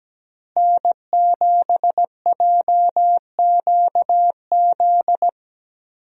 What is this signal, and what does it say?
Morse code, heard as a keyed tone, about 17 wpm, 700 Hz. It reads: N7JQZ
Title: USA Call Signs - 5 Characters - Repeated 3x - 17wpm